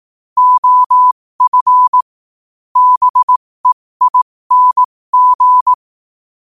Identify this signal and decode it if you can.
Morse code, heard as a keyed tone, about 18 wpm, spaced as for 15 wpm, 1000 Hz. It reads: OF BEING
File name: qotdfiles / 15wpm